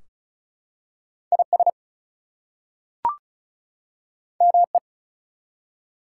Morse code, heard as a keyed tone, about 33 wpm, 700 Hz. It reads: IS ME
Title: Two Letter Words - 35wpm